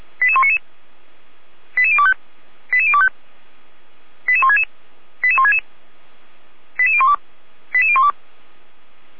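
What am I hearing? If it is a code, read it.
Five-tone selective call (ZVEI), heard as quick runs of five tones, tones of about 70 ms. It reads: 88199 88025 88025 88166 88166 88012 88012
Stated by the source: ZVEI five-tone selective call received on an SDR, Feuerwehr Orlamünde.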